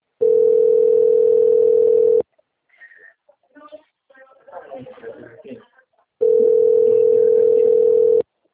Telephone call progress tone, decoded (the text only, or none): ringback tone